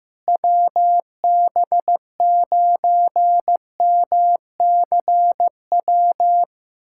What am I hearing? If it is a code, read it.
Morse code, heard as a keyed tone, about 15 wpm, 700 Hz. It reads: WB9MCW
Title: USA Callsign DE Callsign - 6 Characters - 15wpm